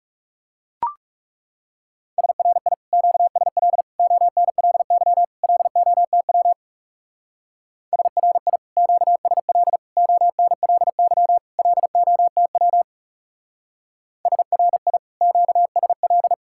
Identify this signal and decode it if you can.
Morse code, heard as a keyed tone, about 36 wpm, 700 Hz. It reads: SRI QSL ONLY LOTW SRI QSL ONLY LOTW SRI QSL
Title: Speed-Racing Common QSO Phrases - 30wpm